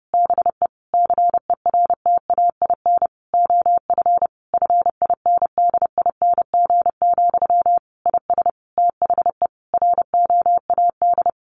Morse code, heard as a keyed tone, about 30 wpm, 700 Hz. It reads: BE CERTAIN OF FINDING, IS THE ROAD
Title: Sentences from Top 1100 Words 30wpm